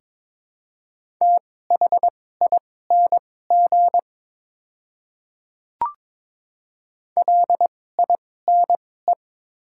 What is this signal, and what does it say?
Morse code, heard as a keyed tone, about 22 wpm, spaced as for 16 wpm, 700 Hz. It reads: THING LINE
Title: 2x Character Spacing - Top 200 Words - 22wpm